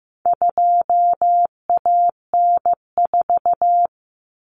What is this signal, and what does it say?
Morse code, heard as a keyed tone, about 15 wpm, 700 Hz. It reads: 2AN4
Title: Common Cut Numbers - 4-Digit 15wpm